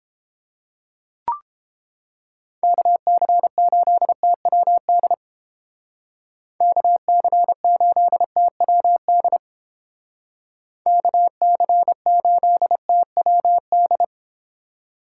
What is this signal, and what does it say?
Morse code, heard as a keyed tone, about 31 wpm, 700 Hz. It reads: KC8TWD KC8TWD KC8TWD